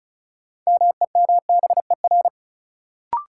Morse code, heard as a keyed tone, about 36 wpm, 700 Hz. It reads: MEMBER
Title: Sets of 3 Words - New Word Drill - 35wpm